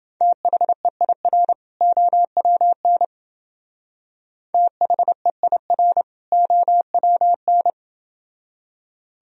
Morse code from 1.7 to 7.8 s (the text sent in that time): OWN THEIR OWN